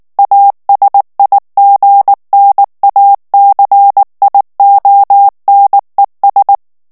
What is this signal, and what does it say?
Morse code, heard as a keyed tone, about 19 wpm, 800 Hz. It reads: ASIGNACIONES
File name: morse mp3 / ESPAGNOL TEXTE EN CLAIR 2 VITESSE 18 MOTS MINUTES